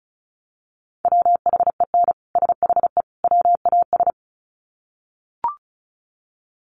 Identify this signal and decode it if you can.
Morse code, heard as a keyed tone, about 34 wpm, 700 Hz. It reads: WHEN SHE WAS
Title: Sets of 3 Words 35wpm